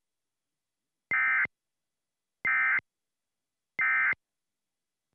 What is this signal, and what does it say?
Emergency Alert System (SAME) tones, heard as harsh buzzing bursts. It is heard from 1.1 to 4.1 s.